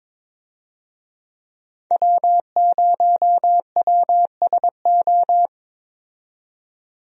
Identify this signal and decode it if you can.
Morse code, heard as a keyed tone, about 22 wpm, 700 Hz. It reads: W0WSO
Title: USA Call Signs - 5 Characters - 22wpm